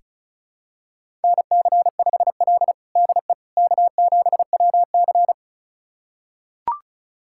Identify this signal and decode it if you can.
Morse code, heard as a keyed tone, about 34 wpm, 700 Hz. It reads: NC5L DE K7WC
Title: USA Callsign DE Callsign - 4 Characters - 35wpm